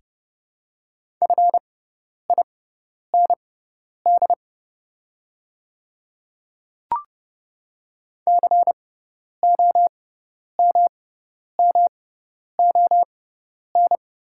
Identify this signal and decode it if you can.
Morse code, heard as a keyed tone, about 31 wpm, spaced as for 10 wpm, 690 Hz. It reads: FIND COMMON
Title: 6x Character Spacing - Top 500 Words - 30wpm